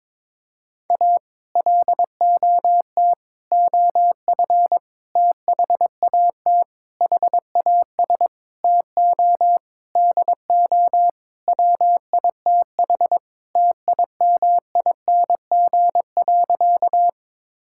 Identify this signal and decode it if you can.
Morse code, heard as a keyed tone, about 22 wpm, 700 Hz. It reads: A LOT OF THAT HAS TO DO WITH TIMING.
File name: practice / 22wpm / Sentences with 8-Word Phrases - 22wpm